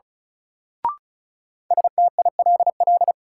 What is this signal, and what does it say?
Morse code, heard as a keyed tone, about 35 wpm, 700 Hz. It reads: STILL
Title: Sentences from Top 300 Words - New Word Drill - 35wpm